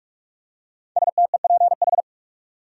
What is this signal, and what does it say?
Morse code, heard as a keyed tone, about 46 wpm, 700 Hz. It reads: STEPH
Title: USA Common Names - Max 5 Letters - 45wpm